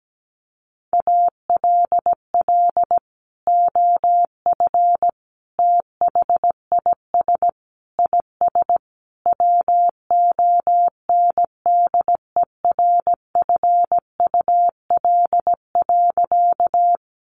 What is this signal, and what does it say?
Morse code, heard as a keyed tone, about 17 wpm, 700 Hz. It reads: ALL OF THIS IS WONDERFUL.
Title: Sentences from Top 1100 Words 17wpm